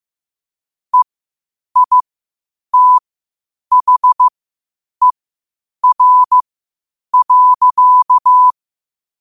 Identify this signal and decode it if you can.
Morse code, heard as a keyed tone, about 15 wpm, spaced as for 8 wpm, 1000 Hz. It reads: EITHER.
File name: qotdfiles / 8wpm